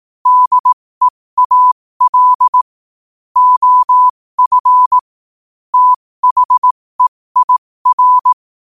Morse code, heard as a keyed tone, about 18 wpm, spaced as for 15 wpm, 1000 Hz. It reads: DEAL OF THEIR
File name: qotdfiles / 15wpm